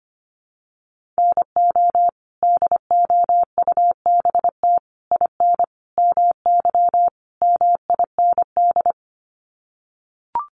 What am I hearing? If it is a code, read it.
Morse code, heard as a keyed tone, about 25 wpm, 700 Hz. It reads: NO DOUBT IN MY MIND
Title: Sets of 5 Words Encore2 25wpm